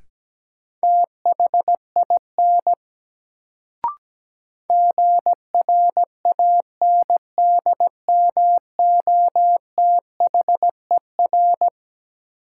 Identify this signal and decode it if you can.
Morse code, heard as a keyed tone, about 17 wpm, 700 Hz. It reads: THIN GRANDMOTHER
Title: Sentences from Top 1200 Words - New Word Drill Alt - 17wpm